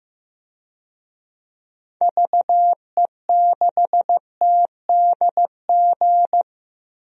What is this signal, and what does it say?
Morse code, heard as a keyed tone, about 15 wpm, 700 Hz. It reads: VE6TDG
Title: DXCC-100 Call Signs - 6 Characters - 15wpm